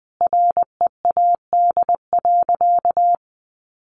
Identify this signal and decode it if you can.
Morse code, heard as a keyed tone, about 20 wpm, 700 Hz. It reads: READ.
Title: Sentences from Top 1000 Words 20wpm